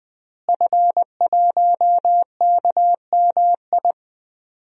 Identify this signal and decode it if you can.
Morse code, heard as a keyed tone, about 20 wpm, 700 Hz. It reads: F1KMI
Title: DXCC-100 Call Signs - 5 Characters - Repeated 2x - 20wpm